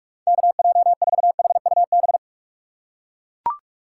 Morse code, heard as a keyed tone, about 43 wpm, 700 Hz. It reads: KJ4HUB
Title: USA Call Signs - 4x Character Spacing - 45wpm